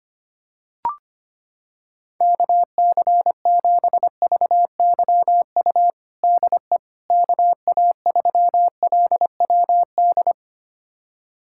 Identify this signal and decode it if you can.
Morse code, heard as a keyed tone, about 25 wpm, 700 Hz. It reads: KC7VYU DE KA3LWD
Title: USA Callsign DE Callsign 25wpm